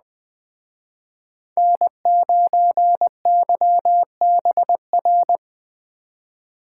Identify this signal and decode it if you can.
Morse code, heard as a keyed tone, about 20 wpm, 700 Hz. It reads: N9YBR
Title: Speed Racing - USA 1x3 Call Signs 17wpm